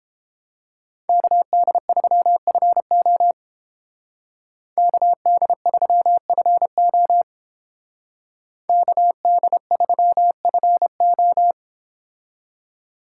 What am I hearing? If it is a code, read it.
Morse code, heard as a keyed tone, about 28 wpm, 700 Hz. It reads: KD3FO KD3FO KD3FO